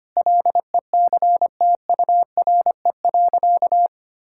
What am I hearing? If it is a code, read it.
Morse code, heard as a keyed tone, about 25 wpm, 700 Hz. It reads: LECTURE.
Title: Sentences from Top 1600 Words 25wpm